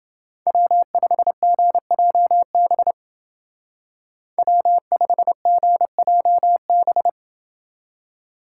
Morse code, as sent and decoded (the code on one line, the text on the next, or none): .-- ..... --. .--- -... / .-- ..... --. .--- -...
W5GJB W5GJB